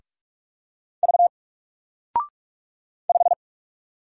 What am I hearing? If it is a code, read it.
Morse code, heard as a keyed tone, about 44 wpm, 700 Hz. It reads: V 5
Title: Single Letter-Number 45wpm